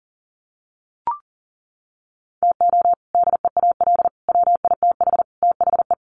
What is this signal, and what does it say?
Morse code, heard as a keyed tone, about 40 wpm, 700 Hz. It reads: TO DEAL WITH THE
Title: Sets of 4 Words 40wpm